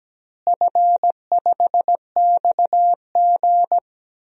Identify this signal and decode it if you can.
Morse code, heard as a keyed tone, about 17 wpm, 700 Hz. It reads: F5XG
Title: DXCC-100 Call Signs - 4 Characters - Repeated 3x - 17wpm